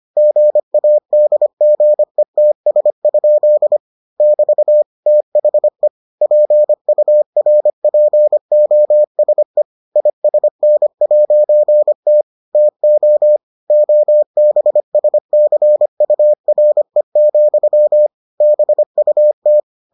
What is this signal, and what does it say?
Morse code, heard as a keyed tone, about 25 wpm, 600 Hz. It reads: GADGETS? <BT> THE PURPOSE ISN'T TO OBSCURE, BUT